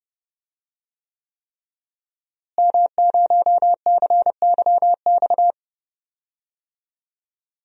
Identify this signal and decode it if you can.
Morse code, heard as a keyed tone, about 30 wpm, 700 Hz. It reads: M0CYX
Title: DXCC-100 Call Signs - 5 Characters - Repeated 3x - 30wpm